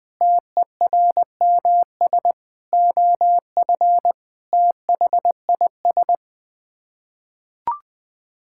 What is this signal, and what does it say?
Morse code, heard as a keyed tone, about 20 wpm, 700 Hz. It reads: TERMS OF THIS
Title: Sets of 9 Words - 20wpm